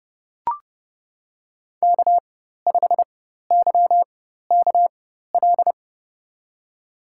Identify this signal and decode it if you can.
Morse code, heard as a keyed tone, about 32 wpm, spaced as for 14 wpm, 700 Hz. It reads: K5YKL
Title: USA Call Signs - 5 Characters - 4x Character Spacing - 30wpm